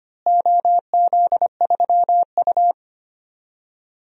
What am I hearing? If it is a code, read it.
Morse code, heard as a keyed tone, about 25 wpm, 700 Hz. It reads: OZ3U
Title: DXCC-100 Call Signs - 4 Characters - Repeated 3x - 25wpm